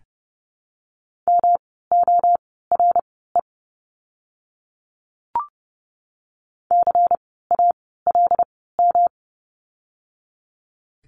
Morse code, heard as a keyed tone, about 30 wpm, spaced as for 17 wpm, 700 Hz. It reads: MORE CALM
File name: course / 30:10wpm / 108 Lesson 18f - Letter m - Words 30wpm